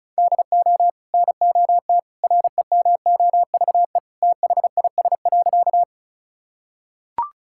Morse code, heard as a keyed tone, about 36 wpm, 700 Hz. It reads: DO NOT REMOVE THIS.
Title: Sentences from Top 1000 Words - Review Part 5 - 35wpm